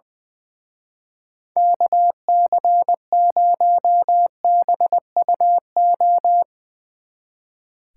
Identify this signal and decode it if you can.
Morse code, heard as a keyed tone, about 22 wpm, 700 Hz. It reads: KC0BUO
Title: Speed Racing - USA 2x3 Call Signs 20wpm